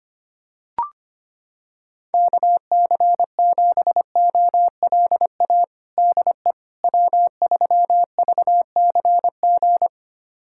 Morse code, heard as a keyed tone, about 25 wpm, 700 Hz. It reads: KC7OLA DE W3VCG
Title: USA Callsign DE Callsign 25wpm